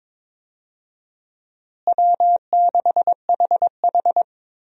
Morse code, heard as a keyed tone, about 22 wpm, 700 Hz. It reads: W6HH
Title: USA Call Signs - 4 Characters - Repeated 2x - 22wpm